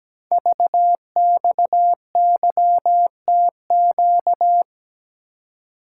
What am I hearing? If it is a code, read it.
Morse code, heard as a keyed tone, about 17 wpm, 700 Hz. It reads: VXYTQ
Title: Code Groups - 5 Letter:Numbers - 17wpm